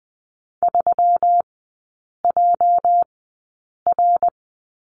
Morse code, heard as a keyed tone, about 20 wpm, spaced as for 8 wpm, 700 Hz. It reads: 3JR
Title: 212 Lesson 34e - Number 8 - Three Characters 20wpm